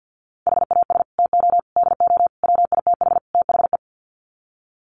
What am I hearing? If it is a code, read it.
Morse code, heard as a keyed tone, about 50 wpm, 700 Hz. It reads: HAS TO DO WITH THE
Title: Sets of 5 Words 50wpm